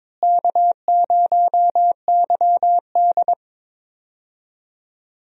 Morse code, heard as a keyed tone, about 22 wpm, 700 Hz. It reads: K0YD
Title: USA Call Signs - 4 Characters - Repeated 3x - 22wpm